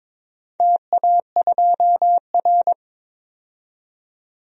Morse code, heard as a keyed tone, about 22 wpm, 700 Hz. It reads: TA2R
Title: DXCC-100 Call Signs - 4 Characters - 22wpm